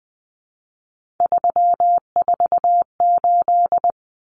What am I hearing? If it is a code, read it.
Morse code, heard as a keyed tone, about 20 wpm, 700 Hz. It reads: 348
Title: Common Cut Numbers - 3-Digit 20wpm